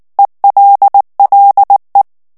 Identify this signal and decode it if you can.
Morse code, heard as a keyed tone, about 19 wpm, 800 Hz. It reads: ELLE
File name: FRANCAIS TEXTE CLAIR 2 VITESSE 18 MOTS MINUTES